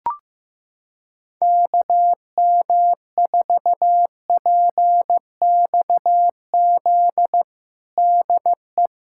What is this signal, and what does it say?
Morse code, heard as a keyed tone, about 15 wpm, 700 Hz. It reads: KM4PXZ DE
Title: USA Callsign DE Callsign - 6 Characters - 15wpm